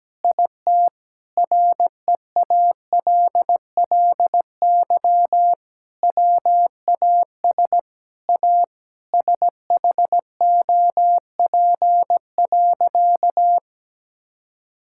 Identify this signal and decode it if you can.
Morse code, heard as a keyed tone, about 17 wpm, 700 Hz. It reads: IT REALLY WAS A SHOP.
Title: Sentences from Top 1600 Words 17wpm